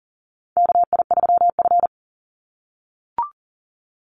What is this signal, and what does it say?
Morse code, heard as a keyed tone, about 41 wpm, 700 Hz. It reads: KI3F
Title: Call Signs - 40wpm